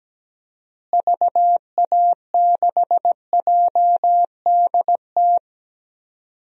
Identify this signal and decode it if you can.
Morse code, heard as a keyed tone, about 17 wpm, 700 Hz. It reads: VA6JDT
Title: DXCC-100 Call Signs - 6 Characters - Repeated 3x - 17wpm